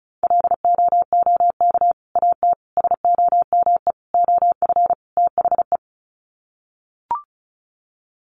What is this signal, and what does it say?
Morse code, heard as a keyed tone, about 34 wpm, 700 Hz. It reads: LOOK AT SOME OF THE
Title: Sets of 5 Words Encore 35wpm